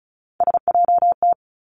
Morse code, heard as a keyed tone, about 36 wpm, 700 Hz. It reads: SJT
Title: Three Challenging Letter-Number Combinations 35wpm